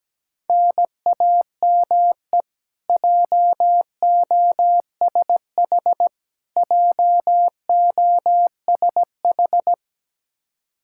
Morse code, heard as a keyed tone, about 17 wpm, 700 Hz. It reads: NAME JOSH JOSH
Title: Practice QSO - Exchange 1 - Beginning - 17wpm